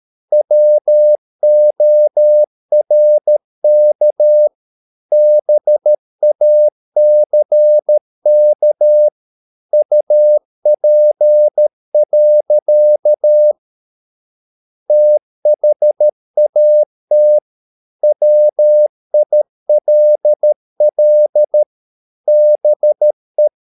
Morse code, heard as a keyed tone, about 13 wpm, 600 Hz. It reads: WORK BACK UP. THAT WILL BE